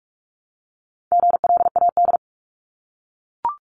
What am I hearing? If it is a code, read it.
Morse code, heard as a keyed tone, about 44 wpm, 700 Hz. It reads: GLAD
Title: Top 751-1000 Words 45wpm